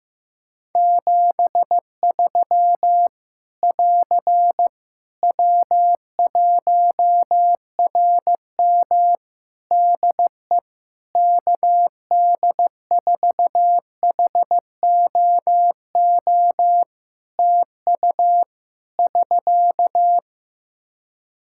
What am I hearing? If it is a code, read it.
Morse code, heard as a keyed tone, about 15 wpm, 700 Hz. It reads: 73 <AR> W1RM DE KD4HOO TU <SK>
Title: Practice QSO - Exchange 4 - Ending - 15wpm